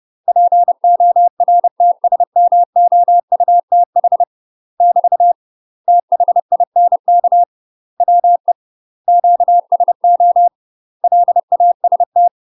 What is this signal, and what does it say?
Morse code, heard as a keyed tone, about 30 wpm, 700 Hz. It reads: PORTSMOUTH <BT> THINK WE QSO LAST
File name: UK MASTER QSOs TRAINING/30WPM